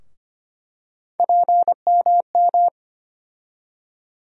Morse code, heard as a keyed tone, about 25 wpm, 700 Hz. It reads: PMM
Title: USA Call Sign Postfixes - 3 Letters 25wpm